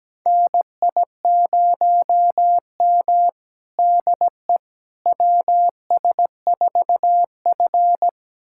Morse code, heard as a keyed tone, about 17 wpm, 700 Hz. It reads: NI0M DE WS4F
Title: USA Callsign DE Callsign - 4 Characters - 17wpm